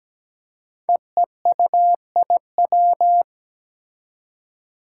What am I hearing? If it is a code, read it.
Morse code, heard as a keyed tone, about 17 wpm, 700 Hz. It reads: EEUIW
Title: Five Challenging Letter-Number Combinations 17wpm